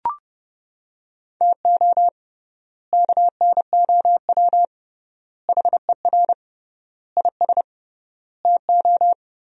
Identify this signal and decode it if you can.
Morse code, heard as a keyed tone, about 32 wpm, 700 Hz. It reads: TO KNOW HER IS TO
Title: 3x Word-Spacing - Sentences from Top 400 Words - 30wpm